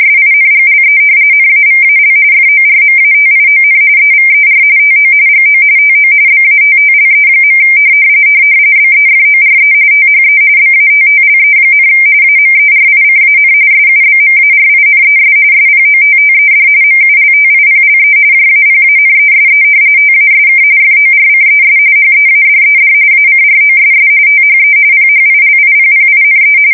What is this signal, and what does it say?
RTTY, about 45.45 baud. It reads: SINGLE NAVAL FORMATION EVER TO GO TO BATTLE. BEFORE THE BATTLE, AMERICAN SUBMARINES SPOTTED MULTIPLE JAPANESE TASK FORCES HEADED ACROSS THE PHILIPPINE
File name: fsk170 / battleshipnj